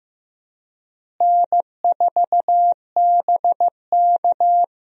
Morse code, heard as a keyed tone, about 15 wpm, 700 Hz. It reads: N4BK
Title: USA Call Signs - 4 Characters - Repeated 2x - 15wpm